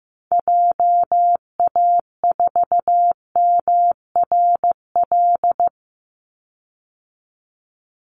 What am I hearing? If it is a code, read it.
Morse code, heard as a keyed tone, about 15 wpm, 700 Hz. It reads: JA4MRL
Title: Call Signs Repeated 3x - 15wpm